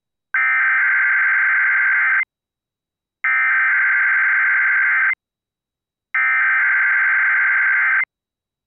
Emergency Alert System (SAME) tones (header bursts), heard as harsh buzzing bursts. It is heard from 0.4 to 8.0 s.